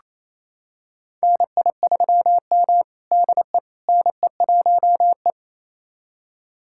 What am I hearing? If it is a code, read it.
Morse code, heard as a keyed tone, about 28 wpm, 700 Hz. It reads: NI3M DE NE1E
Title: USA Callsign DE Callsign - 4 Characters - 28wpm